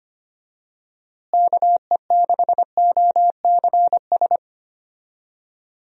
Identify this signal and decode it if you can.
Morse code, heard as a keyed tone, about 25 wpm, 700 Hz. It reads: KE6OCS
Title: USA Call Signs - 6 Characters - Repeated 3x - 25wpm